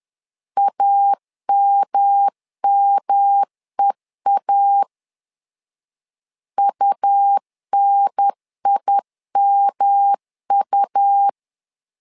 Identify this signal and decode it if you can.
Morse code, heard as a keyed tone, about 11 wpm, 800 Hz. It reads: AMMEA UNIMU
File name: LEZIONI / 3 MWRO / 3 M 10wpm 1828